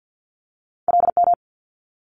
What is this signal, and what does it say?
Morse code, heard as a keyed tone, about 50 wpm, 700 Hz. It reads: LM
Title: Multi-letter Phonemes Quick Fire 50wpm